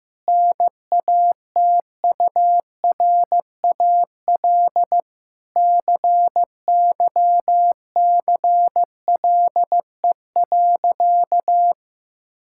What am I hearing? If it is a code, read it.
Morse code, heard as a keyed tone, about 15 wpm, 700 Hz. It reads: NATURAL CYCLE.
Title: Sentences from Top 1000 Words - Review Part 6 - 15wpm